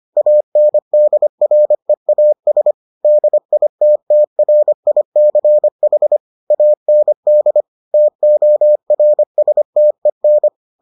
Morse code, heard as a keyed tone, about 25 wpm, 600 Hz. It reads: ANDREAS DITTRICH AND TORSTEN